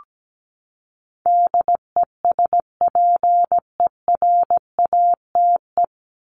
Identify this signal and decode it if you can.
Morse code, heard as a keyed tone, about 17 wpm, 700 Hz. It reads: DESPERATE
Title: Top 1751-2000 Words 17wpm